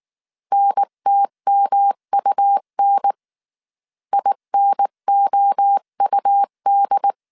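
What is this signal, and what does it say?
Morse code, heard as a keyed tone, about 19 wpm, 800 Hz. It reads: NTMUN INOUD
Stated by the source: Morse code dictation at 18 wpm from the IW3SGT QRS Morse code training course, five-character groups of 300 characters.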